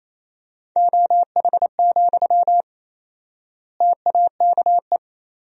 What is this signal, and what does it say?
Morse code, heard as a keyed tone, about 28 wpm, 700 Hz. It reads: OH, TAKE
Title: 4x Word-Spacing - Sentences from Top 500 Words - 28wpm